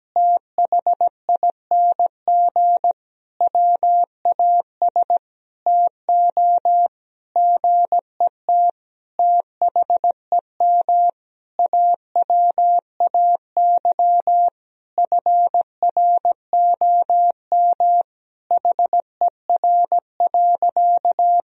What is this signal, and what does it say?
Morse code, heard as a keyed tone, about 17 wpm, 700 Hz. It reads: THING WAS TO GET THEM AWAY FROM HER.
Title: Sentences from Top 1000 Words - Review Part 2 - 17wpm